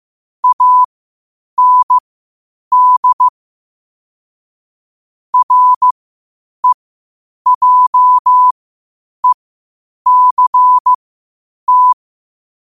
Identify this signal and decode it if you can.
Morse code, heard as a keyed tone, about 15 wpm, spaced as for 8 wpm, 1000 Hz. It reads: AND REJECT
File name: qotdfiles / 8wpm